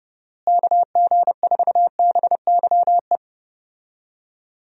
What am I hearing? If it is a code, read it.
Morse code, heard as a keyed tone, about 30 wpm, 700 Hz. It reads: KG4BYE
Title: USA Call Signs - 6 Characters - Repeated 3x - 30wpm